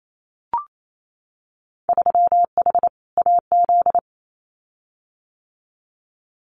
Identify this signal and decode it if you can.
Morse code, heard as a keyed tone, about 28 wpm, 700 Hz. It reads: 3H AZ
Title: Winter Field Day Exchanges Encore 28wpm